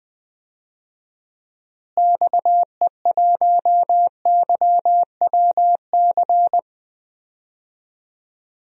Morse code, heard as a keyed tone, about 20 wpm, 700 Hz. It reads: XE1YWC
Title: DXCC-100 Call Signs - 6 Characters - Repeated 3x - 20wpm